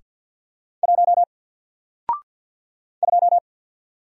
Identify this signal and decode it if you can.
Morse code, heard as a keyed tone, about 48 wpm, 700 Hz. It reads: 1 2
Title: Single Letter-Number Alt 50wpm